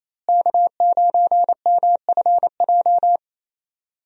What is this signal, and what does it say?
Morse code, heard as a keyed tone, about 27 wpm, 700 Hz. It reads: K9MFJ
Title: USA Call Signs - 5 Characters - Repeated 3x - 28wpm